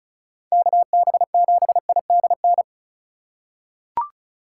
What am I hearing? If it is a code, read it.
Morse code, heard as a keyed tone, about 36 wpm, 690 Hz. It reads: KB7IDN
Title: USA Call Signs - 6 Characters - 3x Character Spacing - 35wpm